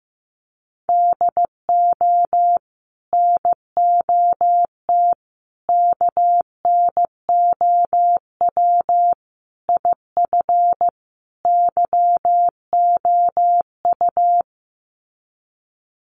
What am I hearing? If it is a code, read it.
Morse code, heard as a keyed tone, about 15 wpm, 700 Hz. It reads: DO NOT KNOW IF YOU
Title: Sets of 5 Words 15wpm